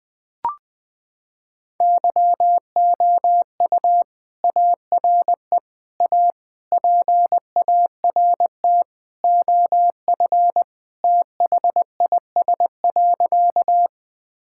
Morse code, heard as a keyed tone, about 20 wpm, 700 Hz. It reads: YOU ARE A PART OF THIS.
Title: Sentences with 3-Word Phrases - 20wpm